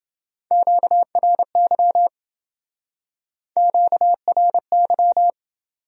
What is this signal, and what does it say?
Morse code, heard as a keyed tone, about 28 wpm, 700 Hz. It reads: QRY QRY